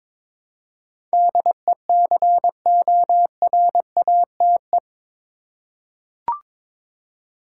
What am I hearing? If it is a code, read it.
Morse code, heard as a keyed tone, about 22 wpm, 700 Hz. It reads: DECORATE
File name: practice / 22wpm / Eight Letter Words - 22wpm